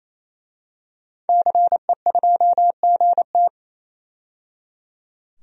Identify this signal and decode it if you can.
Morse code, heard as a keyed tone, about 28 wpm, 700 Hz. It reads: CE2GT
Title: DXCC-100 Call Signs - 5 Characters - 28wpm